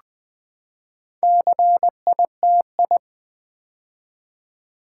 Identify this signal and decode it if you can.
Morse code, heard as a keyed tone, about 20 wpm, 700 Hz. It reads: CITI
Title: Build A Word - 4 Letters At A Time - 11-Letter Words - 20wpm